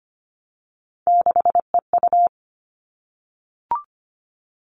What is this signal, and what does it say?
Morse code, heard as a keyed tone, about 25 wpm, 700 Hz. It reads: 6EU
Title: All Cut Numbers - 3-Digit 25wpm